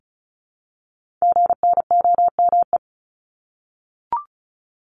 Morse code, heard as a keyed tone, about 36 wpm, 700 Hz. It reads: GNOME
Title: Multi-letter Phonemes Repeated 3x 35wpm